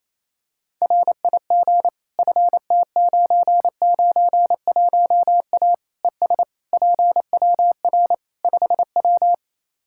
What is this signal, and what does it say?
Morse code, heard as a keyed tone, about 28 wpm, 700 Hz. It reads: RIG FT991A ES PWR 5W
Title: Practice QSO - Exchange 2 - Info - 28wpm